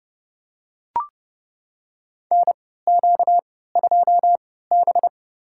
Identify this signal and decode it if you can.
Morse code, heard as a keyed tone, about 32 wpm, spaced as for 17 wpm, 700 Hz. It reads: NQ2B
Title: USA Call Signs - 4 Characters - 3x Character Spacing - 30wpm